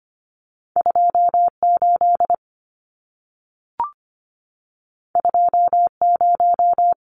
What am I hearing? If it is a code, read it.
Morse code, heard as a keyed tone, about 25 wpm, 700 Hz. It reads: 28 20
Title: Numbers - 2-Digit 25wpm